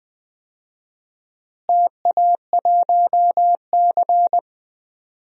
Morse code, heard as a keyed tone, about 21 wpm, 700 Hz. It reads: TA1C
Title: DXCC-100 Call Signs - 4 Characters - Repeated 3x - 20wpm